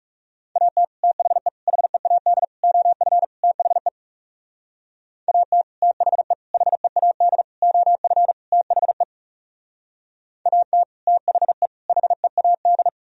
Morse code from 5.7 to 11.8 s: THE HEAD OF THE AT THE

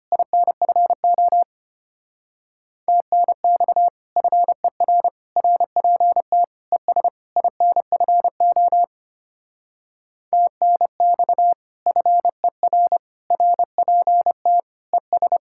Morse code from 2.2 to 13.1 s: TNX FER RPT ES INFO TNX FER

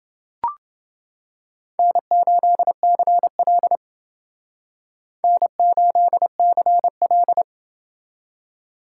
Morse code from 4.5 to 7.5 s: N8CL

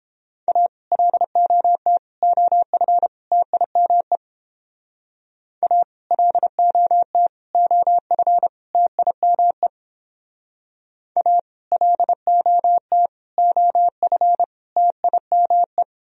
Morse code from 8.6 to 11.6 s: TIME A

